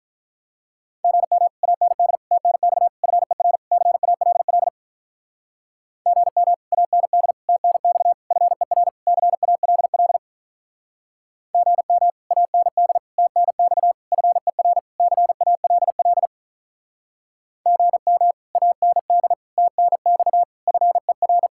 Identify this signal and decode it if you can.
Morse code, heard as a keyed tone, about 44 wpm, 700 Hz. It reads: GM AND TNX FER CALL GM AND TNX FER CALL GM AND TNX FER CALL GM AND TNX FER